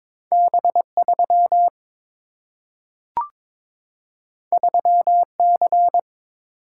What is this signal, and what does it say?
Morse code, heard as a keyed tone, about 22 wpm, 700 Hz. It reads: B 3 3 C
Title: Single-Letter Single-Number 22wpm